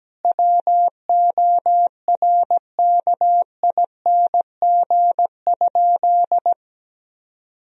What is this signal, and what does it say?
Morse code, heard as a keyed tone, about 17 wpm, 700 Hz. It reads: WORKING?
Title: Sentences from Top 1000 Words - Review Part 4 - 17wpm